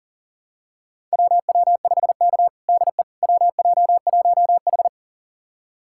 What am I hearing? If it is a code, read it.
Morse code, heard as a keyed tone, about 40 wpm, 700 Hz. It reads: WW5K DE WJ1H
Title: USA Callsign DE Callsign - 4 Characters - 40wpm